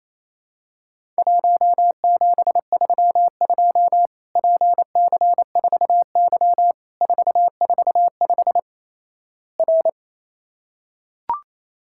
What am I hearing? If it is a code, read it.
Morse code, heard as a keyed tone, about 28 wpm, 700 Hz. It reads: 1732 PC4Y 445 R
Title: Worked All Europe DX Contest - QTC Exchange - 28wpm